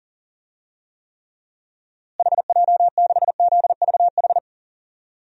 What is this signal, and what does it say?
Morse code, heard as a keyed tone, about 40 wpm, 700 Hz. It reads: HJ6ZVH
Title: DXCC-100 Call Signs - 6 Characters - Repeated 3x - 40wpm